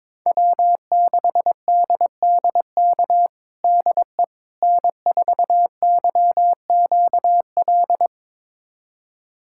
Morse code, heard as a keyed tone, about 22 wpm, 700 Hz. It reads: W6DDK DE N4YQL
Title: USA Callsign DE Callsign - 5 Characters - 22wpm